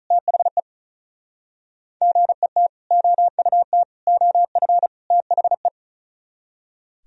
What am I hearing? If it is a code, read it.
Morse code, heard as a keyed tone, about 47 wpm, 700 Hz. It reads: THE GET OUT OF THE